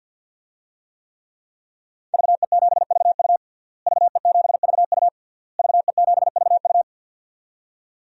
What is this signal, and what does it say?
Morse code, heard as a keyed tone, about 50 wpm, 700 Hz. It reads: VE7VU VE7VU VE7VU
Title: DXCC-100 Call Signs - 5 Characters - Repeated 3x - 50wpm